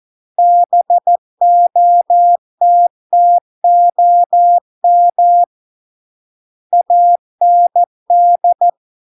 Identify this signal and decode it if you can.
Morse code, heard as a keyed tone, about 14 wpm, 700 Hz. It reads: BOTTOM AND